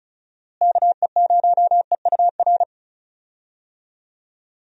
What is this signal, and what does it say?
Morse code, heard as a keyed tone, about 36 wpm, 700 Hz. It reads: KE0EUR
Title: USA Call Signs - 6 Characters - Repeated 2x - 35wpm